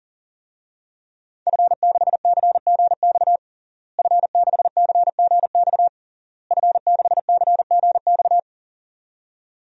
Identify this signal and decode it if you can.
Morse code, heard as a keyed tone, about 40 wpm, 700 Hz. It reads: F6CGX F6CGX F6CGX
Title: DXCC-100 Call Signs - 5 Characters - Repeated 3x - 40wpm